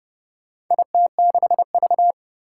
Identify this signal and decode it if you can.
Morse code, heard as a keyed tone, about 30 wpm, 700 Hz. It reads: IT6V
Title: Four Challenging Letter-Number Combinations 30wpm